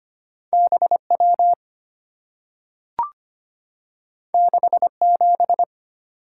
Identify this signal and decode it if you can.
Morse code, heard as a keyed tone, about 25 wpm, 700 Hz. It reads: BW 67
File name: Two Letter-Number Combinations - 25wpm